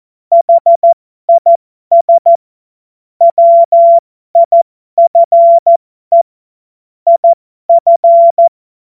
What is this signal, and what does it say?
Morse code, heard as a keyed tone, about 14 wpm, spaced as for 12 wpm, 680 Hz. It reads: HIS WIFE IF
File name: Morsekurs / Witze